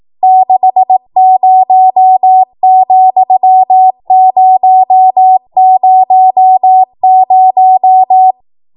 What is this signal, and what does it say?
Morse code, heard as a keyed tone, about 18 wpm, 750 Hz. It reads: 60,000
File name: Morse / 18WPM